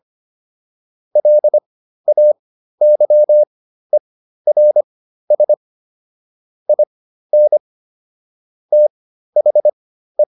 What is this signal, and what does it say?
Morse code, heard as a keyed tone, about 25 wpm, spaced as for 13 wpm, 600 Hz. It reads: LAYERS IN THE